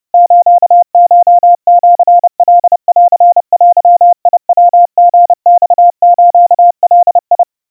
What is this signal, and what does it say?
Morse code (Japanese Wabun code), heard as a keyed tone, about 30 wpm, 700 Hz. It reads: スコシカンデヤリマスガ
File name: kawa / Morse sample / kamyu 110 cpm 700 Hz Crispy